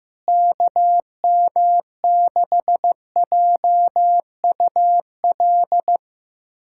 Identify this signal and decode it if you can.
Morse code, heard as a keyed tone, about 15 wpm, 700 Hz. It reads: KM6JUL
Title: USA Callsign DE Callsign 15wpm